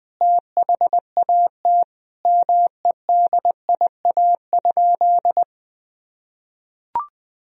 Morse code, heard as a keyed tone, about 20 wpm, 700 Hz. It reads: THAT MEDIA?
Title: Sentences from Top 1000 Words - Review Part 4 - 20wpm